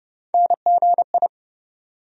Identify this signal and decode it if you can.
Morse code, heard as a keyed tone, about 30 wpm, 700 Hz. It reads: NGI
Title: USA Call Sign Postfixes - 3 Letters 30wpm